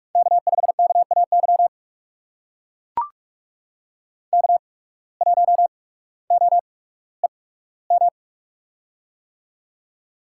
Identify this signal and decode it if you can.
Morse code, heard as a keyed tone, about 44 wpm, spaced as for 19 wpm, 700 Hz. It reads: K5KAY K1OEM
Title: USA Call Signs - 5 Characters - 8x Character Spacing - 45wpm